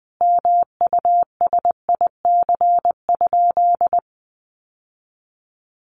Morse code, heard as a keyed tone, about 20 wpm, 700 Hz. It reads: MUSIC?
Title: Sentences from Top 1000 Words 20wpm